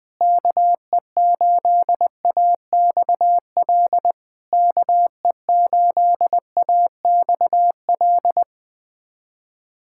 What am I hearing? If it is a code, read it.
Morse code, heard as a keyed tone, about 21 wpm, 700 Hz. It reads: KE8AXL KE8AXL
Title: SOTA QSO Practice 20wpm